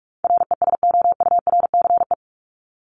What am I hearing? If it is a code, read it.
Morse code, heard as a keyed tone, about 44 wpm, 700 Hz. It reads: RESOURCE
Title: Top 1001-1250 Words 45wpm